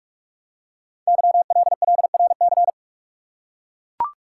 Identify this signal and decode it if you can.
Morse code, heard as a keyed tone, about 44 wpm, 700 Hz. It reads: YLLRC
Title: Code Groups - 5 Letters - 45wpm